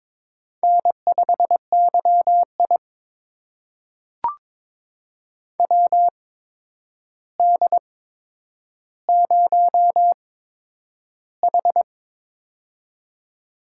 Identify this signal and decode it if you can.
Morse code, heard as a keyed tone, about 22 wpm, spaced as for 9 wpm, 700 Hz. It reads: N5YI WD0H
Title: USA Call Signs - 4 Characters - 8x Character Spacing - 22wpm